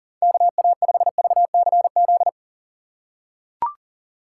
Morse code, heard as a keyed tone, about 40 wpm, 700 Hz. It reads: KA5VCZ
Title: USA Call Signs - 6 Characters - 3x Character Spacing - 40wpm